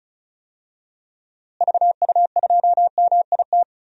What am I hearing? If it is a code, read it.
Morse code, heard as a keyed tone, about 36 wpm, 700 Hz. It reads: VU2MIT
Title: DXCC-100 Call Signs - 6 Characters - Repeated 3x - 35wpm